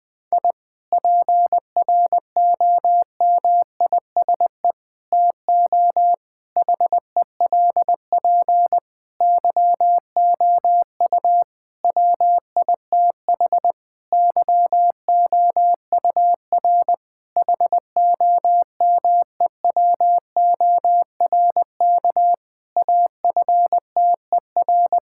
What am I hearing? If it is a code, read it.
Morse code, heard as a keyed tone, about 20 wpm, 700 Hz. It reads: I PROMISE TO HELP YOU WITH YOUR HOMEWORK AFTER
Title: Difficult Sentences with Hints - 20wpm